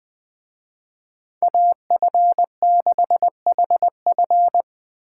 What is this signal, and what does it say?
Morse code, heard as a keyed tone, about 24 wpm, 700 Hz. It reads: AF6HF
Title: Speed Racing - USA 2x2 Call Signs 20wpm